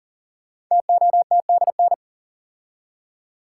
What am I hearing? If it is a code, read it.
Morse code, heard as a keyed tone, about 40 wpm, 700 Hz. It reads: TOTDN
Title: Code Groups - 5 Letters - 40wpm